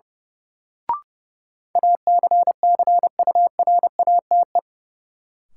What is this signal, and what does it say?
Morse code, heard as a keyed tone, about 32 wpm, 700 Hz. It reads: ACCURATE
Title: Sentences from Top 1200 Words - New Word Drill Alt - 30wpm